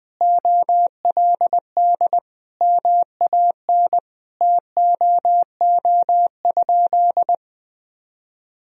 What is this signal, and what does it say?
Morse code, heard as a keyed tone, about 20 wpm, 700 Hz. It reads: OLD MAN TOO?
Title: Sentences from Top 1000 Words - Review Part 4 - 20wpm